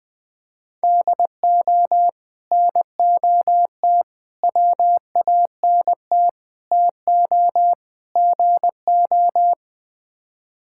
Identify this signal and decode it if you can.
Morse code, heard as a keyed tone, about 20 wpm, 700 Hz. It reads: DO NOT WANT TO GO
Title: Build A Phrase - 5-Word Phrases - 20wpm